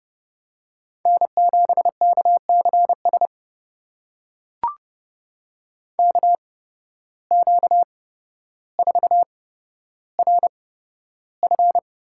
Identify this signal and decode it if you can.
Morse code, heard as a keyed tone, about 30 wpm, spaced as for 13 wpm, 700 Hz. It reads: N7KCS KQ4RF
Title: USA Call Signs - 5 Characters - 8x Character Spacing - 30wpm